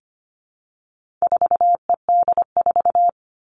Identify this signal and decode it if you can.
Morse code, heard as a keyed tone, about 25 wpm, 700 Hz. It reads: 4ED4
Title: All Cut Numbers - 4-Digit 25wpm